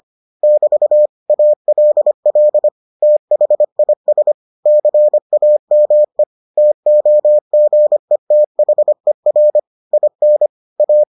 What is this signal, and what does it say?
Morse code, heard as a keyed tone, about 25 wpm, 600 Hz. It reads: <BT> ALL THIS CAME TOGETHER IN A